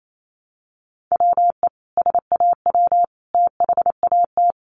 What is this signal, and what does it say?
Morse code, heard as a keyed tone, about 28 wpm, 700 Hz. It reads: WE SAW THAT
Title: Sentences from Top 1200 Words 28wpm